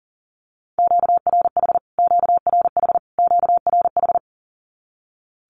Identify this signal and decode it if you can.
Morse code, heard as a keyed tone, about 40 wpm, 700 Hz. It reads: QRH QRH QRH
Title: Q-Codes Repeated 3x 40wpm